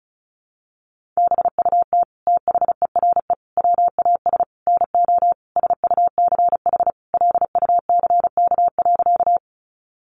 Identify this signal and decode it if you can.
Morse code, heard as a keyed tone, about 34 wpm, 700 Hz. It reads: BUT THERE WAS NO SUCH LUCK.
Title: Sentences from Top 1500 Words 35wpm